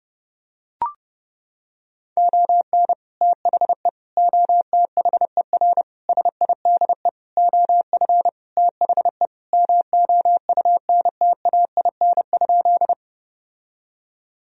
Morse code, heard as a keyed tone, about 30 wpm, 700 Hz. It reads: ON THE OTHER SIDE OF THE MOUNTAIN?
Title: Sentences from Top 1000 Words - Review Part 1 - 30wpm